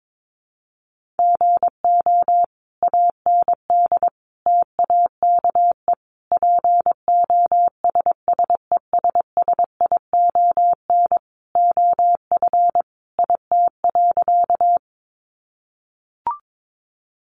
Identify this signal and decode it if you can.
Morse code, heard as a keyed tone, about 22 wpm, 700 Hz. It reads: GO AND TAKE POSSESSION OF IT.
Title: Sentences from Top 1400 Words 22wpm